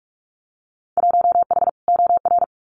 Morse code, heard as a keyed tone, about 46 wpm, 700 Hz. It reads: 1H OR
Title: Winter Field Day Exchanges Encore 45wpm